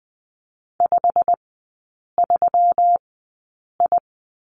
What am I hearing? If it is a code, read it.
Morse code, heard as a keyed tone, about 20 wpm, spaced as for 8 wpm, 700 Hz. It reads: 53I
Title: 138 Lesson 23e - Letter f - Three Characters 20wpm